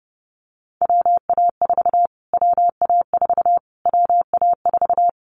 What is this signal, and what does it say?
Morse code, heard as a keyed tone, about 30 wpm, 700 Hz. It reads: WA4 WA4 WA4
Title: Callsign Prefixes for Top 25 DXCC - Repeated 3x - 30wpm